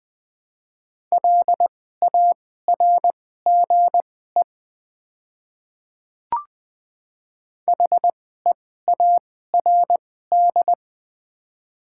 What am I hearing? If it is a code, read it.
Morse code, heard as a keyed tone, about 20 wpm, spaced as for 14 wpm, 700 Hz. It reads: LARGE HEARD
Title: 2x Character Spacing - Top 500 Words - 20wpm